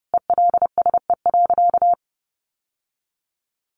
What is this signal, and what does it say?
Morse code, heard as a keyed tone, about 30 wpm, 700 Hz. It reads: ELSE.
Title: Sentences from Top 1100 Words 30wpm